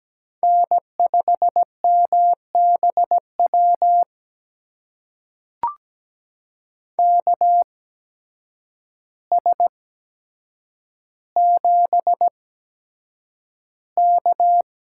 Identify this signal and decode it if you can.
Morse code, heard as a keyed tone, about 17 wpm, spaced as for 8 wpm, 700 Hz. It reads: N5MBW KS7K
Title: USA Call Signs - 8x Character Spacing - 17wpm